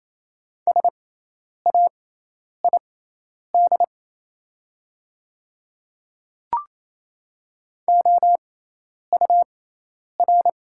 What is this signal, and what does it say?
Morse code, heard as a keyed tone, about 28 wpm, spaced as for 10 wpm, 700 Hz. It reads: SAID OUR
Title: 6x Character Spacing - Top 500 Words - 28wpm